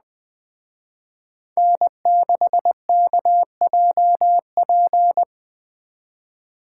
Morse code, heard as a keyed tone, about 21 wpm, 700 Hz. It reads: N6KJP
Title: Speed Racing - USA 1x3 Call Signs 17wpm